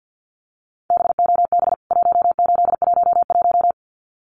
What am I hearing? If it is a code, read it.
Morse code, heard as a keyed tone, about 50 wpm, 700 Hz. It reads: BOB 1811
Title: CWT Exchange 50wpm